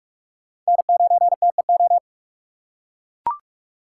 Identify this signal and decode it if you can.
Morse code, heard as a keyed tone, about 41 wpm, 700 Hz. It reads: N9TEO
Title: USA Call Signs - 5 Characters - 4x Character Spacing - 45wpm